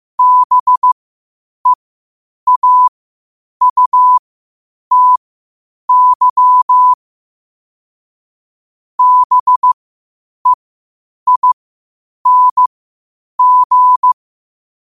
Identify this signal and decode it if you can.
Morse code, heard as a keyed tone, about 15 wpm, spaced as for 8 wpm, 1000 Hz. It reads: BEAUTY BEING
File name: qotdfiles / 8wpm